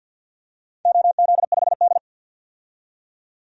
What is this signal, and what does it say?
Morse code, heard as a keyed tone, about 50 wpm, 700 Hz. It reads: OZ5D
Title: DXCC-100 Call Signs - 4 Characters - 50wpm